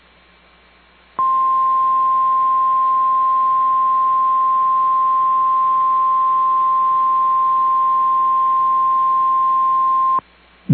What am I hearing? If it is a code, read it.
Emergency Alert System (SAME) tones (the 1050 Hz attention tone). It is heard from 1.1 to 10.3 s.
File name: EAS Logs / KMNR